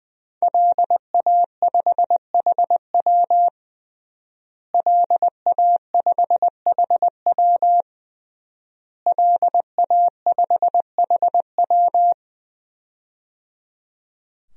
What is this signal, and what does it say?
Morse code, heard as a keyed tone, about 20 wpm, 700 Hz. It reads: LA5HW LA5HW LA5HW
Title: DXCC-100 Call Signs - 5 Characters - Repeated 3x - 20wpm